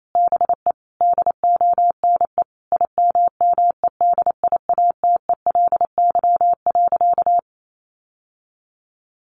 Morse code, heard as a keyed tone, about 28 wpm, 700 Hz. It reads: BE DONE IMMEDIATELY.
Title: Sentences from Top 1100 Words 28wpm